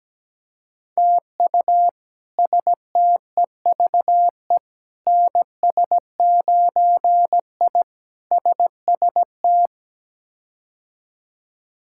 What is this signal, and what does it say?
Morse code, heard as a keyed tone, about 17 wpm, 700 Hz. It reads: TU STEVE NS9I SST
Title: K1USN SST QSO Practice - 17wpm